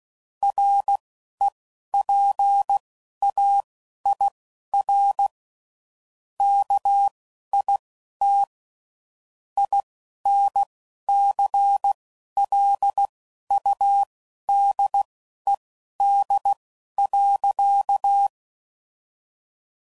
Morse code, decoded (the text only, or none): REPAIR KIT INCLUDED.